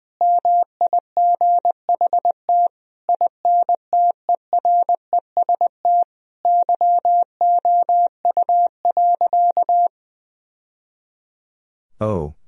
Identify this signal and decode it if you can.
Morse code, heard as a keyed tone, about 20 wpm, 700 Hz. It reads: MIGHT INTEREST YOU.
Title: Sentences from Top 1000 Words - Review Part 2 - 20wpm